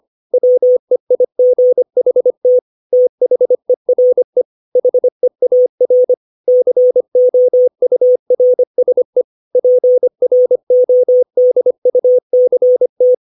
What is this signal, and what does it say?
Morse code, heard as a keyed tone, about 25 wpm, 500 Hz. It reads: WEIGHT THERE HEAR COURSE PRODUCT
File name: 632 words / 632 Words 25WPM